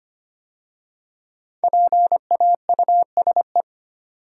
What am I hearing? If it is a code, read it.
Morse code, heard as a keyed tone, about 25 wpm, 700 Hz. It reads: PAUSE
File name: 5 Letter Words - First 3 Letters Given - 25wpm